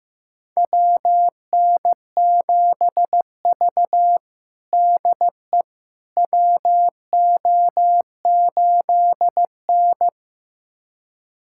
Morse code, heard as a keyed tone, about 15 wpm, 700 Hz. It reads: WN7V DE WO8N
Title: USA Callsign DE Callsign - 4 Characters - 15wpm